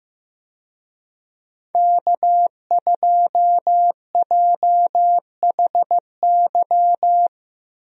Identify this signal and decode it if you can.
Morse code, heard as a keyed tone, about 15 wpm, 700 Hz. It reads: K2JHY
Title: USA Call Signs - 5 Characters - Repeated 2x - 15wpm